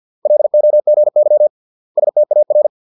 Morse code, heard as a keyed tone, about 50 wpm, 600 Hz. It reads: LOGY STAR